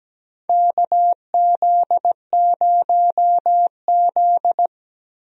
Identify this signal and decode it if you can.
Morse code, heard as a keyed tone, about 17 wpm, 700 Hz. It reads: KZ0Z
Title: USA Call Signs - 4 Characters - 17wpm